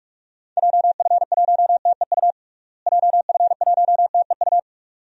Morse code, heard as a keyed tone, about 42 wpm, 690 Hz. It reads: JF1TEU JF1TEU
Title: DXCC-100 Call Signs - 6 Characters - Repeated 2x - 45wpm